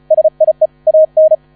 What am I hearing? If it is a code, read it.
Morse code, heard as a keyed tone, about 35 wpm, 630 Hz. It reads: SIE AN
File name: 35wpm